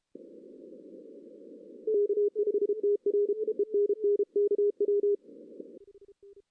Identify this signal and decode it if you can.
Morse code, heard as a keyed tone, about 31 wpm, 410 Hz. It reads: K5TR ENN KW
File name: 2016 arrl dx cw / 2016-ARRL-DX-CW-K5TR- - 0-3